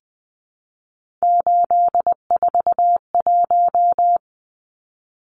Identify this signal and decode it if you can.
Morse code, heard as a keyed tone, about 20 wpm, 700 Hz. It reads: 841
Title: Numbers - 3-Digit 20wpm